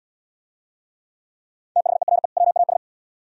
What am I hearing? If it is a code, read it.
Morse code, heard as a keyed tone, about 75 wpm, 700 Hz. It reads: THERE WAS